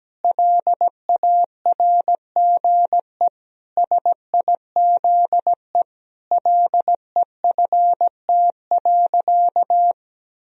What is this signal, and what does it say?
Morse code, heard as a keyed tone, about 17 wpm, 700 Hz. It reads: LARGE SIZE LEFT.
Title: Sentences from Top 1000 Words - Review Part 2 - 17wpm